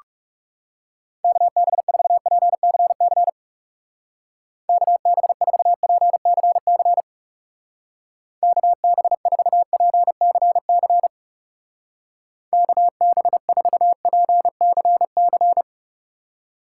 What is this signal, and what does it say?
Morse code, heard as a keyed tone, about 36 wpm, 700 Hz. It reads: KB4PCC KB4PCC KB4PCC KB4PCC
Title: Speed Racing - USA Call Signs 30wpm